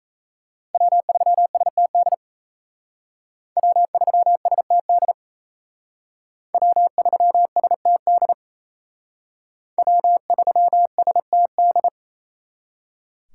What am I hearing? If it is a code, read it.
Morse code, heard as a keyed tone, about 34 wpm, 700 Hz. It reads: W3STD W3STD W3STD W3STD
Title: Speed Racing - USA 1x3 Call Signs 28wpm